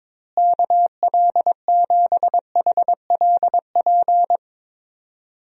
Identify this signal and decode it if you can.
Morse code, heard as a keyed tone, about 22 wpm, 700 Hz. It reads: KL7HLP
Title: DXCC-100 Call Signs - 6 Characters - Repeated 3x - 22wpm